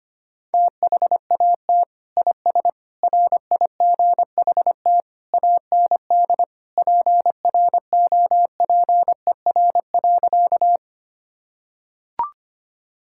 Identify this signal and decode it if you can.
Morse code, heard as a keyed tone, about 25 wpm, 700 Hz. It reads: THAT IS RIGHT AND PROPER.
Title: Sentences from Top 1000 Words - Review Part 5 - 25wpm